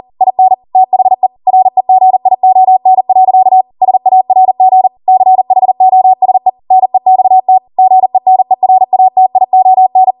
Morse code, heard as a keyed tone, about 40 wpm, 750 Hz. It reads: IN THE REGION. SARG CHOSE NEXT GENERATION